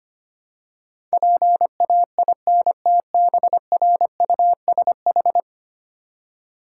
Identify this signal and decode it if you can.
Morse code, heard as a keyed tone, about 25 wpm, 690 Hz. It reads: PAINTBRUSH
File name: practice / 25wpm / Easy Words with Hints - 25wpm